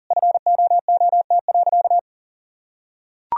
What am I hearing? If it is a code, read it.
Morse code, heard as a keyed tone, about 40 wpm, 700 Hz. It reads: FOOT.
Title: Sentences from Top 1000 Words - Review Part 3 - 40wpm